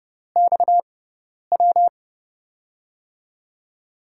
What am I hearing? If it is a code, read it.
Morse code, heard as a keyed tone, about 30 wpm, spaced as for 10 wpm, 700 Hz. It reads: XW
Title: USA Call Signs - 6 Characters - 6x Character Spacing - 30wpm